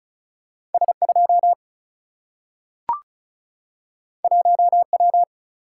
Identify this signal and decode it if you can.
Morse code, heard as a keyed tone, about 36 wpm, 700 Hz. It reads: S2 1W
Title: Single-Letter Single-Number 35wpm